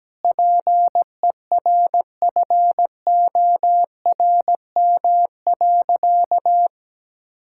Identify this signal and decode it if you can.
Morse code, heard as a keyed tone, about 17 wpm, 700 Hz. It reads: PERFORM.
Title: Sentences from Top 1000 Words - Review Part 5 - 17wpm